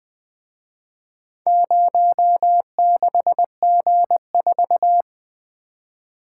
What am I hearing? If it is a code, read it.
Morse code, heard as a keyed tone, about 24 wpm, 700 Hz. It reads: 06G4
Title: Speed-Racing - Four Letter-Number Combinations - 20wpm